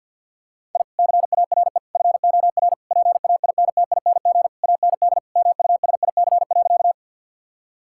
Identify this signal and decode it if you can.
Morse code, heard as a keyed tone, about 47 wpm, 700 Hz. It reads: I CARE FOR PAINTING AND MUSIC.
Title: Sentences from Top 1000 Words - Review Part 6 - 50wpm